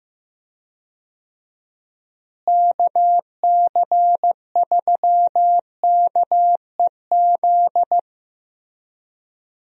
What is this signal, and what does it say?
Morse code, heard as a keyed tone, about 15 wpm, 700 Hz. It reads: KC3KEZ
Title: USA Call Signs - 6 Characters - Repeated 3x - 15wpm